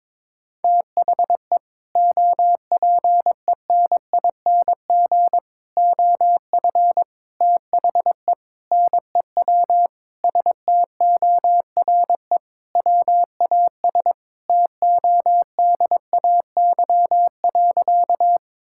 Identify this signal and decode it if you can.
Morse code, heard as a keyed tone, about 22 wpm, 700 Hz. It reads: THE OPENING OF THE NEW STORE WAS TODAY.
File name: practice / 22wpm / Sentences with 3-Word Phrases - 22wpm